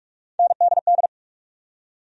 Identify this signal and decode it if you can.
Morse code, heard as a keyed tone, about 42 wpm, 700 Hz. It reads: NDD
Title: USA Call Sign Postfixes - 3 Letters 45wpm